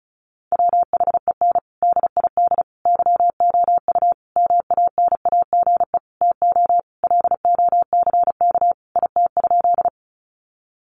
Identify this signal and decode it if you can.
Morse code, heard as a keyed tone, about 34 wpm, 700 Hz. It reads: WHEN DID YOU MANAGE TO LOCK IT?
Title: Sentences from Top 1400 Words 35wpm